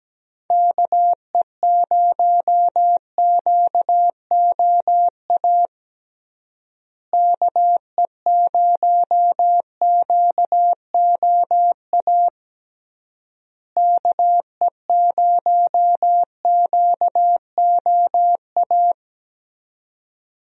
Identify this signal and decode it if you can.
Morse code, heard as a keyed tone, about 17 wpm, 700 Hz. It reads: KE0QOA KE0QOA KE0QOA
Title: USA Call Signs - 6 Characters - Repeated 3x - 17wpm